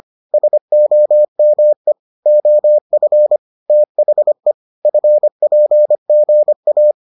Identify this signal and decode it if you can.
Morse code, heard as a keyed tone, about 25 wpm, 600 Hz. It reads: SOME OF THE FPGA